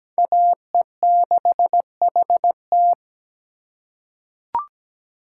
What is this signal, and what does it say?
Morse code, heard as a keyed tone, about 17 wpm, 700 Hz. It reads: AE6HT
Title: USA Call Signs - 8x Character Spacing - 17wpm